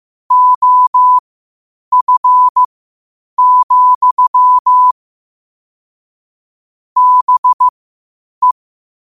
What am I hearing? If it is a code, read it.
Morse code, heard as a keyed tone, about 15 wpm, spaced as for 8 wpm, 1000 Hz. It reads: OF, BE
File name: qotdfiles / 8wpm